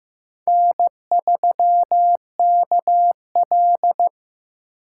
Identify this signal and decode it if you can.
Morse code, heard as a keyed tone, about 15 wpm, 700 Hz. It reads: N3KL
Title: USA Call Signs - 4 Characters - 8x Character Spacing - 15wpm